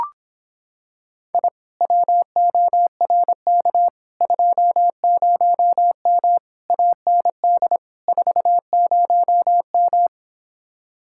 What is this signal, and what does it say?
Morse code, heard as a keyed tone, about 26 wpm, 700 Hz. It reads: I WORK 20M AND 40M